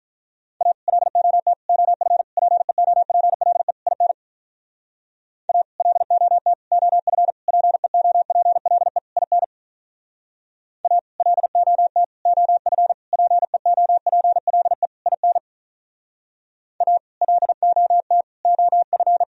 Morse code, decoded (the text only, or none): A LOT OF PEOPLE IN A LOT OF PEOPLE IN A LOT OF PEOPLE IN A LOT OF